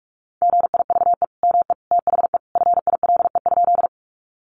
Morse code, heard as a keyed tone, about 44 wpm, 700 Hz. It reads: GIVE ME THE FILE?
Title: Sentences from Top 1200 Words 45wpm